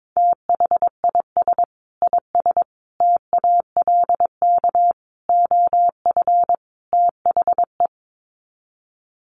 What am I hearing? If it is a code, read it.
Morse code, heard as a keyed tone, about 22 wpm, 700 Hz. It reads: THIS IS TALK OF THE
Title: Sets of 5 Words 22wpm